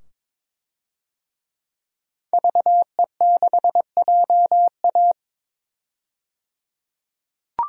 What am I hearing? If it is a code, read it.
Morse code, heard as a keyed tone, about 22 wpm, 700 Hz. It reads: VE6JA
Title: DXCC-100 Call Signs - 5 Characters - 22wpm